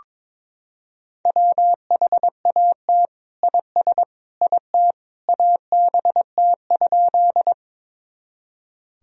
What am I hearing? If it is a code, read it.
Morse code, heard as a keyed tone, about 22 wpm, 700 Hz. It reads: WHAT IS IT ABT?
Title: Sentences from Top 100 Words and Abbreviations 22wpm